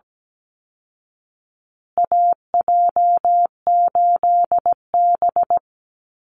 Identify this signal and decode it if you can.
Morse code, heard as a keyed tone, about 17 wpm, 700 Hz. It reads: AJ8B
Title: Call Signs Repeated 3x - 17wpm